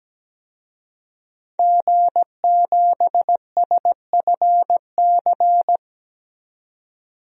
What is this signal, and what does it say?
Morse code, heard as a keyed tone, about 17 wpm, 700 Hz. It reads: G7SFC
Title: DXCC-100 Call Signs - 5 Characters - 17wpm